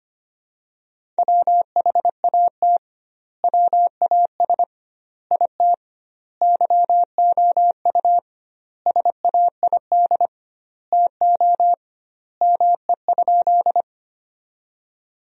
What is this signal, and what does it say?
Morse code, heard as a keyed tone, about 25 wpm, 700 Hz. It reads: WHAT WAS IT YOU SAID TO ME?
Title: 2x Word-Spacing - Sentences from Top 200 Words - 25wpm